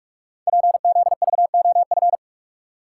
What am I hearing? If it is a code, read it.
Morse code, heard as a keyed tone, about 44 wpm, 700 Hz. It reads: PZVOF
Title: Code Groups - 5 Letters - 45wpm